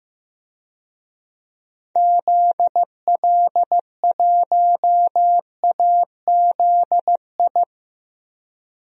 Morse code, heard as a keyed tone, about 15 wpm, 700 Hz. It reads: ZL1AZI
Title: DXCC-100 Call Signs - 6 Characters - 15wpm